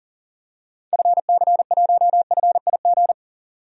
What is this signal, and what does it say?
Morse code, heard as a keyed tone, about 38 wpm, 700 Hz. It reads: FC1FIG
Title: DXCC-100 Call Signs - 6 Characters - 40wpm